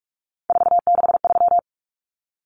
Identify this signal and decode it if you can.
Morse code, heard as a keyed tone, about 44 wpm, 700 Hz. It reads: 463
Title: Common Cut Numbers - 3-Digit 45wpm